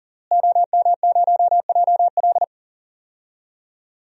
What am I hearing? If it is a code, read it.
Morse code, heard as a keyed tone, about 40 wpm, 700 Hz. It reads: OM0JL
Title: DXCC-100 Call Signs - 5 Characters - Repeated 3x - 40wpm